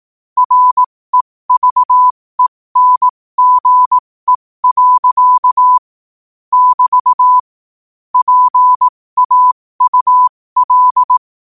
Morse code, heard as a keyed tone, about 18 wpm, spaced as for 15 wpm, 1000 Hz. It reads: REVENGE. <BT> PAUL
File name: qotdfiles / 15wpm